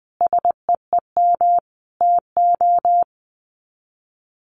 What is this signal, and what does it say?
Morse code, heard as a keyed tone, about 20 wpm, 700 Hz. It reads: SEEM TO
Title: Sets of 5 Words Encore2 20wpm